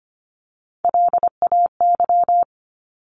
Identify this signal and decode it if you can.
Morse code, heard as a keyed tone, about 25 wpm, 700 Hz. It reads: LAY
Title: Top 500 Words 25wpm